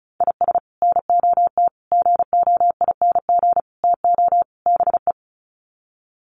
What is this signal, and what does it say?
Morse code, heard as a keyed tone, about 34 wpm, 700 Hz. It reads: IS NOT GOING TO BE
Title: Sets of 5 Words 35wpm